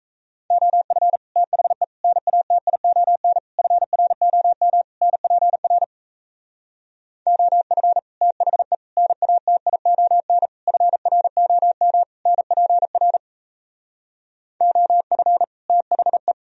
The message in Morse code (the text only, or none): OF THE NATION FROM NPR OF THE NATION FROM NPR OF THE